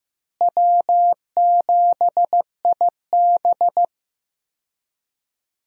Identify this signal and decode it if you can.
Morse code, heard as a keyed tone, about 15 wpm, 700 Hz. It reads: W7IB
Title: USA Call Signs - 4 Characters - Repeated 2x - 15wpm